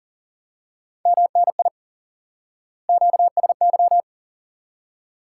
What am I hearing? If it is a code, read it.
Morse code, heard as a keyed tone, about 40 wpm, 700 Hz. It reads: MNI QSY
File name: IQE Course 3 - Lesson 89 - YAGI - Two Elements - 40wpm